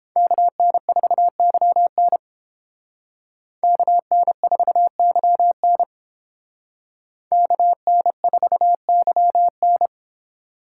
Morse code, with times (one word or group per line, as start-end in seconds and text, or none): 0.2-2.2 KN4YN
3.6-5.8 KN4YN
7.3-9.9 KN4YN